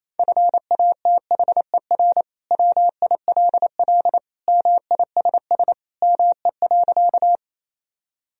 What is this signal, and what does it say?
Morse code, heard as a keyed tone, about 28 wpm, 700 Hz. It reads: FATHER WILL MISS ME.
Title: Sentences from Top 1000 Words - Review Part 3 - 28wpm